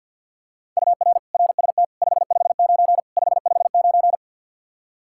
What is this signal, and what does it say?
Morse code, heard as a keyed tone, about 52 wpm, 700 Hz. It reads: UR RST 559 559
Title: Practice QSO - Exchange 1 - Beginning - 50wpm